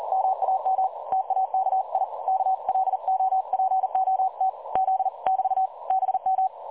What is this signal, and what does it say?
Morse code, heard as a keyed tone, about 38 wpm, 760 Hz. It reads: MAM TAKI KLOPOT Z <BT> BM
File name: dlazaawansowanych / nagraniazpasma / websdr recording 3565.0kHz